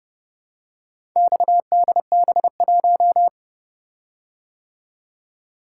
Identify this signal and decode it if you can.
Morse code, heard as a keyed tone, about 30 wpm, 700 Hz. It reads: XDB1